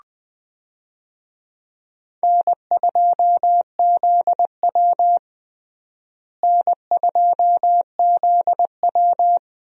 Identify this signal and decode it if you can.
Morse code, heard as a keyed tone, about 21 wpm, 700 Hz. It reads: N2ZW N2ZW
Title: USA Call Signs - 4 Characters - Repeated 2x - 20wpm